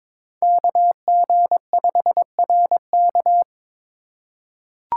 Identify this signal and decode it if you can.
Morse code, heard as a keyed tone, about 22 wpm, 700 Hz. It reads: KG5RK
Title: USA Call Signs - 8x Character Spacing - 22wpm